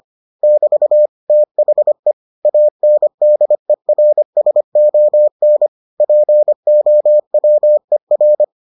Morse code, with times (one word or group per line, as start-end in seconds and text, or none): 0.4-1.1 <BT>
1.3-2.1 THE
2.5-5.7 ANDERSON
6.0-8.4 POWER